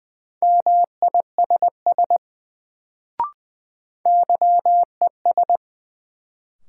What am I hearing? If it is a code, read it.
Morse code, heard as a keyed tone, about 20 wpm, 700 Hz. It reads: MISS YES
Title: Sentences from Top 500 Words - New Word Drill - 20wpm